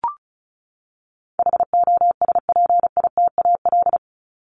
Morse code, heard as a keyed tone, about 34 wpm, 700 Hz. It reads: HOSPITAL
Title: Top 1751-2000 Words 35wpm